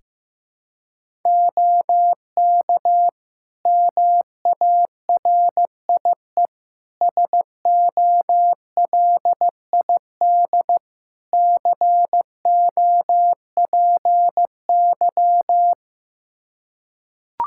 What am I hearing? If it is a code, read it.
Morse code, heard as a keyed tone, about 15 wpm, 700 Hz. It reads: OK MARIE SOLID COPY
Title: Practice QSO - Exchange 3 - More Info - 15wpm